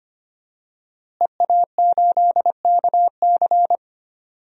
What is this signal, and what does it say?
Morse code, heard as a keyed tone, about 25 wpm, 700 Hz. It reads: EA8KC
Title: DXCC-100 Call Signs - 5 Characters - Repeated 2x - 25wpm